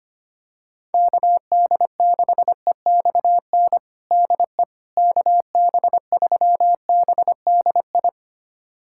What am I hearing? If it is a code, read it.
Morse code, heard as a keyed tone, about 25 wpm, 700 Hz. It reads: KD6EXN DE KB3BDI
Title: USA Callsign DE Callsign 25wpm